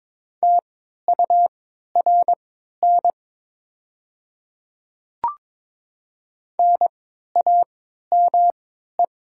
Morse code, heard as a keyed tone, about 22 wpm, spaced as for 12 wpm, 700 Hz. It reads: TURN NAME
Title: 3x Character Spacing - Top 200 Words - 22wpm